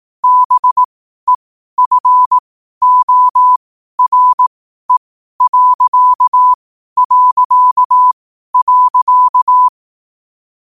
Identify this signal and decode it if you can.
Morse code, heard as a keyed tone, about 18 wpm, spaced as for 12 wpm, 1000 Hz. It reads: BEFORE...
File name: qotdfiles / 12wpm